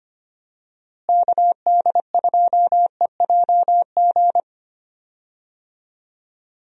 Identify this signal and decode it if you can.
Morse code, heard as a keyed tone, about 25 wpm, 700 Hz. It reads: KD2EJG
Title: USA Call Signs - 6 Characters - Repeated 3x - 25wpm